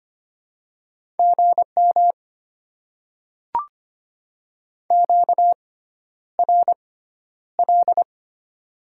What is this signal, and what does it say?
Morse code, heard as a keyed tone, about 25 wpm, spaced as for 11 wpm, 700 Hz. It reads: GM QRL
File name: Top 90 QSO Elements - 6x Character Spacing - 25wpm